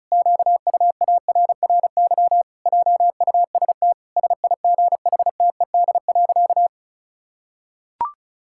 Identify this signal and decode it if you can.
Morse code, heard as a keyed tone, about 34 wpm, 700 Hz. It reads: QUARRY JUST SIGHTED.
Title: Pangram Challenge v2 - 35wpm